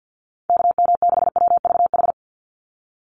Morse code, heard as a keyed tone, about 50 wpm, 700 Hz. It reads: KM6WVH
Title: USA Call Signs 50wpm